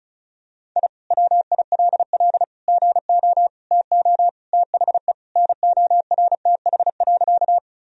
Morse code, heard as a keyed tone, about 34 wpm, 700 Hz. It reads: I WILL GO TO THE NORTH.
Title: Sentences from Top 1000 Words - Review Part 1 - 35wpm